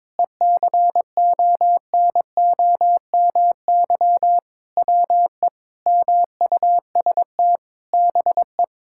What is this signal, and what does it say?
Morse code, heard as a keyed tone, about 22 wpm, 700 Hz. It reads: ECONOMY WE MUST BE
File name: practice / 22wpm / Sentences with 8-Word Phrases - 22wpm